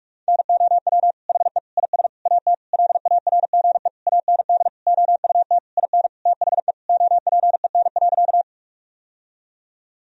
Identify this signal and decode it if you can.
Morse code, heard as a keyed tone, about 46 wpm, 700 Hz. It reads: NOW HE IS AT LARGE AND OUT IN THE OPEN.
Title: Sentences from Top 1000 Words - Review Part 1 - 45wpm